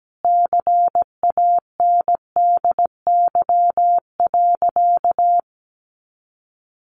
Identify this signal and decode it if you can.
Morse code, heard as a keyed tone, about 17 wpm, 700 Hz. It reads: CANDY.
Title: Easy Phrases 17wpm